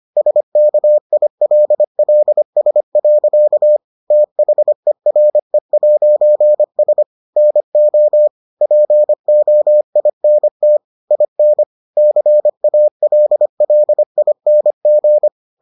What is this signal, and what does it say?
Morse code, heard as a keyed tone, about 25 wpm, 600 Hz. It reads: SKILLS. THERE'S NO POINT IN CALLING